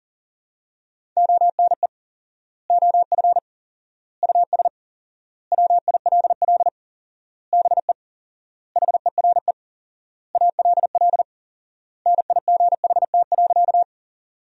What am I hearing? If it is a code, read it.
Morse code, heard as a keyed tone, about 37 wpm, 700 Hz. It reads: ONE OF US WILL BE HERE ALL NIGHT.
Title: 4x Word-Spacing - Sentences from Top 300 Words - 40wpm